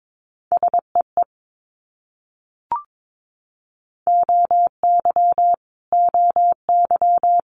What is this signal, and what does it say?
Morse code, heard as a keyed tone, about 22 wpm, 700 Hz. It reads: SEE OY OY
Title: Multi-letter Phonemes Repeated 3x 22wpm